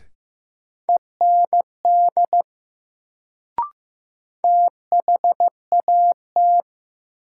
Morse code, heard as a keyed tone, about 15 wpm, 700 Hz. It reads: END THAT
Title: Sets of 2 Words - New Word Drill - 15wpm